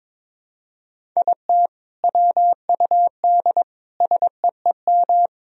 Morse code, heard as a keyed tone, about 22 wpm, 700 Hz. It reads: IT WUD SEEM
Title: Sentences from Top 300 Words and Abbreviations 22wpm